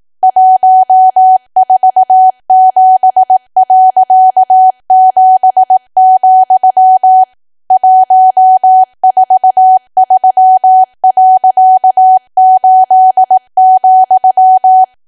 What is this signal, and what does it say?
Morse code, heard as a keyed tone, about 18 wpm, 750 Hz. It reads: 147.7, 143.8,
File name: cwa / W1AW-Level2 / 18WPM